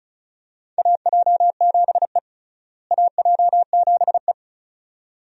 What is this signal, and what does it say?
Morse code, heard as a keyed tone, about 34 wpm, 700 Hz. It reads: AJ7E AJ7E
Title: USA Call Signs - 4 Characters - Repeated 2x - 35wpm